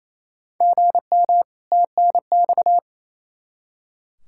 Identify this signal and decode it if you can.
Morse code, heard as a keyed tone, about 28 wpm, 700 Hz. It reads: GM TNX
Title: Practice QSO - Exchange 1 - Beginning - 28wpm